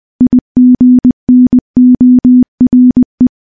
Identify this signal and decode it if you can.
Morse code, heard as a keyed tone, about 21 wpm, 250 Hz. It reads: IGNORE
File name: morse-training-20WPM